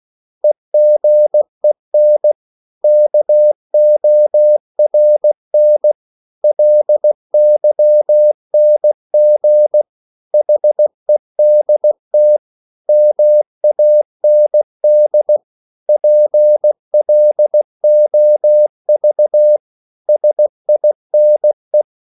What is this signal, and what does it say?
Morse code, heard as a keyed tone, about 16 wpm, 600 Hz. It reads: EGEN KORN LYNG HEDT MAND PLOV SINE
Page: Korte danske ord 16wpm